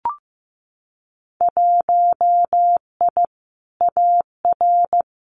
Morse code, heard as a keyed tone, about 15 wpm, 700 Hz. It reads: 1I AR
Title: Winter Field Day Exchanges Encore 15wpm